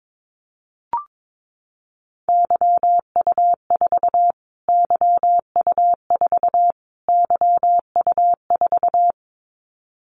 Morse code, heard as a keyed tone, about 22 wpm, 700 Hz. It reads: YU4 YU4 YU4
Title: Callsign Prefixes for Top 25 DXCC - Repeated 3x - 22wpm